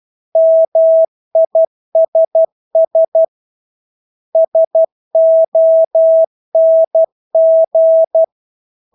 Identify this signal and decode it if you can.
Morse code, heard as a keyed tone, about 12 wpm, 650 Hz. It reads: MISS SONG
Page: Engelske ord 12 wpm